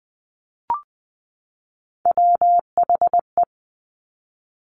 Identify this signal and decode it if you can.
Morse code, heard as a keyed tone, about 20 wpm, 700 Hz. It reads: WHE
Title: Top Three Letter Combinations in Words 20wpm